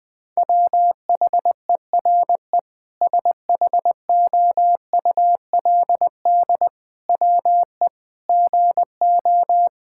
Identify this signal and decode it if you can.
Morse code, heard as a keyed tone, about 20 wpm, 700 Hz. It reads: WHERE SHOULD WE GO
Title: Sentences from Top 1000 Words - Review Part 5 - 20wpm